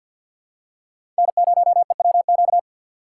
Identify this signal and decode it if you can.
Morse code, heard as a keyed tone, about 53 wpm, 700 Hz. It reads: N0EWQ
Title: USA Call Signs - 5 Characters - 2x Character Spacing - 50wpm